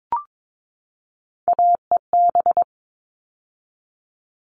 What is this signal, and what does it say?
Morse code, heard as a keyed tone, about 22 wpm, 700 Hz. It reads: AEB
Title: All Cut Numbers - 3-Digit 22wpm